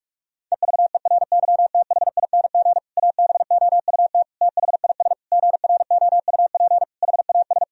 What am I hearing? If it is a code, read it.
Morse code, heard as a keyed tone, about 42 wpm, 700 Hz. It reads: EVERYTHING ABOUT THIS GROUP HAS
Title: Sentences from Top 1000 Words - Review Part 1 - 45wpm